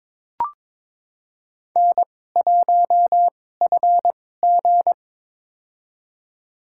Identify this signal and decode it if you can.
Morse code, heard as a keyed tone, about 23 wpm, spaced as for 16 wpm, 700 Hz. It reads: N1FG
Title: USA Call Signs - 4 Characters - 2x Character Spacing - 22wpm